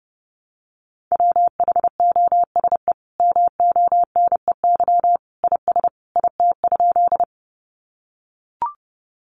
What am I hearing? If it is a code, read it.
Morse code, heard as a keyed tone, about 30 wpm, 700 Hz. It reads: WHOSE MONEY IS IT?
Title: Sentences from Top 1200 Words 30wpm